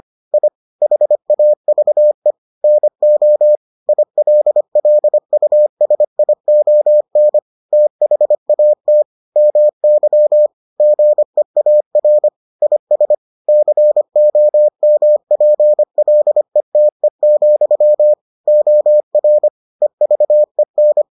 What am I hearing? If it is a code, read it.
Morse code, heard as a keyed tone, about 25 wpm, 600 Hz. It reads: I HAVE NO ILLUSION THAT MY GEAR IS COMPLETE, OR EVEN